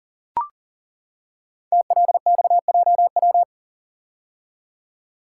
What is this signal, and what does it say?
Morse code, heard as a keyed tone, about 40 wpm, 700 Hz. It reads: TLXJW
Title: Code Groups - 5 Letters - 40wpm